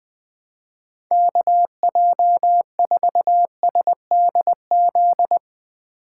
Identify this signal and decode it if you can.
Morse code, heard as a keyed tone, about 21 wpm, 700 Hz. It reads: KJ4SDZ
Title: Speed Racing - USA 2x3 Call Signs 17wpm